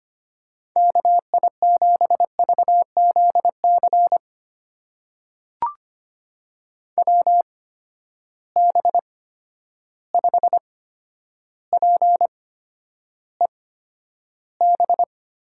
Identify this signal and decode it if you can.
Morse code, heard as a keyed tone, about 25 wpm, spaced as for 11 wpm, 700 Hz. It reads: KI7VZC WB5PEB
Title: USA Call Signs - 6 Characters - 8x Character Spacing - 25wpm